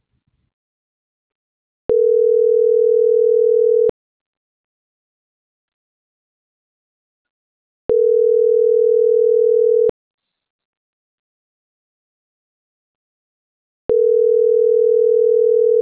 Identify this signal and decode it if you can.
Ringback tone (North America), 440 + 480 Hz.